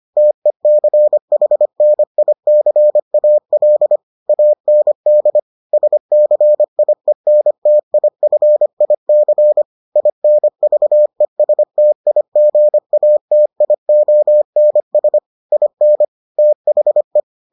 Morse code, heard as a keyed tone, about 25 wpm, 600 Hz. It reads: TECHNICAL AND SCIENTIFIC INVESTIGATIONS IN THE